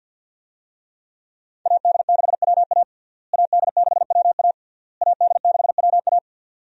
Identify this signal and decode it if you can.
Morse code, heard as a keyed tone, about 49 wpm, 700 Hz. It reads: AD6WA AD6WA AD6WA
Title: USA Call Signs - 5 Characters - Repeated 3x - 50wpm